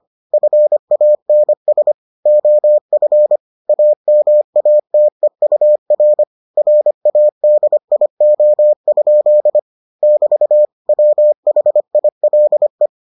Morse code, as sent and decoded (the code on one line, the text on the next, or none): ..-. .- -. ... / --- ..-. / .- -- .- - . ..- .-. / .-. .- -.. .. --- ..--.. / -...- / .-- .... .. .-.. .
FANS OF AMATEUR RADIO? <BT> WHILE